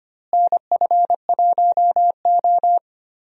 Morse code, heard as a keyed tone, about 25 wpm, 700 Hz. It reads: NF1O
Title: USA Call Signs - 4 Characters - Repeated 2x - 25wpm